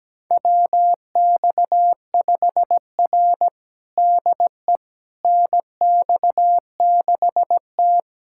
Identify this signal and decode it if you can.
Morse code, heard as a keyed tone, about 17 wpm, 700 Hz. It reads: WX5R DE NX6T
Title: USA Callsign DE Callsign - 4 Characters - 17wpm